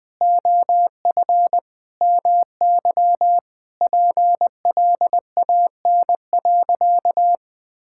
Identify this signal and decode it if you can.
Morse code, heard as a keyed tone, about 20 wpm, 700 Hz. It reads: OF MY PLAN.
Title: Sentences with 2-Word Phrases - Spoken Beginning - 20wpm